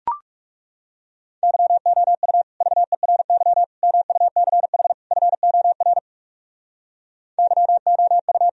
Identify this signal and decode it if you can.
Morse code, heard as a keyed tone, about 37 wpm, 700 Hz. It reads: YOU VERY MUCH FOR YOU
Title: Speed-Racing - Sets of 4 Words 30wpm